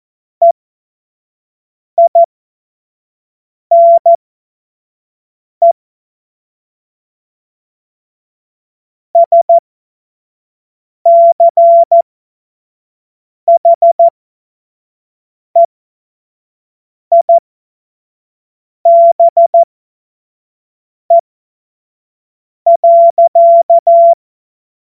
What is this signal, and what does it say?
Morse code, heard as a keyed tone, about 14 wpm, spaced as for 5 wpm, 680 Hz. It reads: EINE SCHEIBE.
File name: Morsekurs / Witze / Witz 10. August 2021, 25 F70